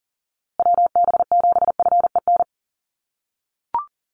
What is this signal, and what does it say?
Morse code, heard as a keyed tone, about 40 wpm, 700 Hz. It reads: WB7FEN
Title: USA Call Signs Repeated 3x 40wpm